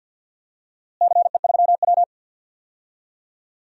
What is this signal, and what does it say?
Morse code, heard as a keyed tone, about 47 wpm, 700 Hz. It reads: CE3W
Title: DXCC-100 Call Signs - 4 Characters - Repeated 3x - 50wpm